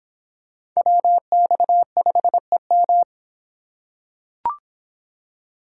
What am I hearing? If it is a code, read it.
Morse code, heard as a keyed tone, about 25 wpm, 700 Hz. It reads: WX5EM